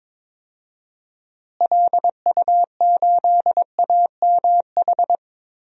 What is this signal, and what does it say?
Morse code, heard as a keyed tone, about 22 wpm, 700 Hz. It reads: LU8AMH
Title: DXCC-100 Call Signs - 6 Characters - Repeated 2x - 22wpm